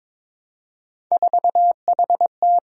Morse code, heard as a keyed tone, about 22 wpm, 700 Hz. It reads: 4HT
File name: practice / 22wpm / Three Letter-Number Combinations - 22wpm